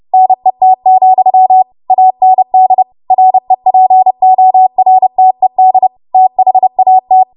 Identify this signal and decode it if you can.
Morse code, heard as a keyed tone, about 30 wpm, 750 Hz. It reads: NET, AND REPORTED THAT
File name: Morse / 30WPM